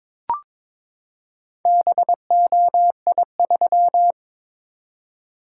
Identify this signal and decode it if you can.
Morse code, heard as a keyed tone, about 22 wpm, 700 Hz. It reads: BOI3
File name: practice / 22wpm / Four Letter-Number Combinations - 22wpm